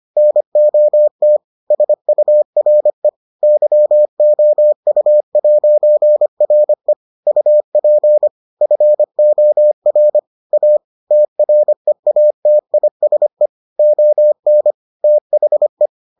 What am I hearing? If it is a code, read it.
Morse code, heard as a keyed tone, about 25 wpm, 600 Hz. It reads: NOT SURE YOU'RE UP FOR A TREATISE ON THE